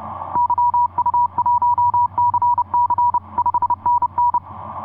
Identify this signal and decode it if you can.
Morse code, heard as a keyed tone, about 30 wpm, 970 Hz. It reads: YU1CC 5NN
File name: od5nj yu1cc etc 40m cw 14jan2010 1719